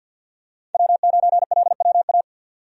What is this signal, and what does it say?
Morse code, heard as a keyed tone, about 50 wpm, 700 Hz. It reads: W9LWA
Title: USA Call Signs - 5 Characters - 50wpm